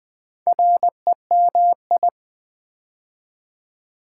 Morse code, heard as a keyed tone, about 20 wpm, 700 Hz. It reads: REMI
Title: Build A Word - 2 Letters At A Time - 8-Letter Words - 20wpm